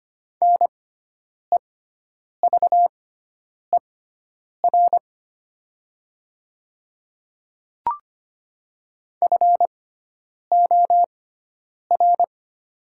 Morse code, heard as a keyed tone, about 25 wpm, spaced as for 9 wpm, 700 Hz. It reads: NEVER FOR
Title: 6x Character Spacing - Top 300 Words - 25wpm